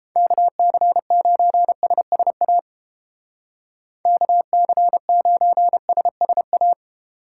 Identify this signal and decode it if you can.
Morse code, heard as a keyed tone, about 29 wpm, 700 Hz. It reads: KC9SSA KC9SSA